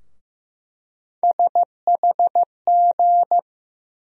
Morse code, heard as a keyed tone, about 16 wpm, 700 Hz. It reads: SHG
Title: USA Call Sign Postfixes - 3 Letters 15wpm